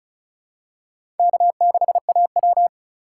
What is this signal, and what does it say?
Morse code, heard as a keyed tone, about 36 wpm, 700 Hz. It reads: K6AW
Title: USA Call Signs - 4 Characters - 3x Character Spacing - 35wpm